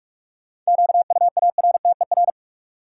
Morse code, heard as a keyed tone, about 46 wpm, 700 Hz. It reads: QUARTER
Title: Sentences from Top 1300 Words - New Word Drill - 45wpm